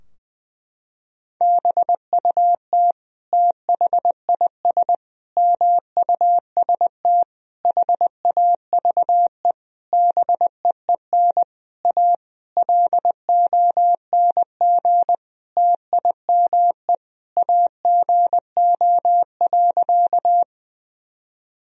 Morse code, heard as a keyed tone, about 20 wpm, 700 Hz. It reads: BUT THIS MUST HAVE BEEN A LONG TIME AGO.
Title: Sentences from Top 1000 Words - Review Part 3 - 20wpm